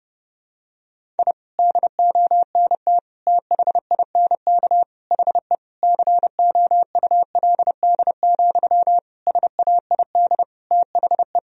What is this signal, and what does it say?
Morse code, heard as a keyed tone, about 30 wpm, 700 Hz. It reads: I DONT THINK HE COULD, SAID THE
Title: Sentences from Top 1000 Words - Review Part 4 - 30wpm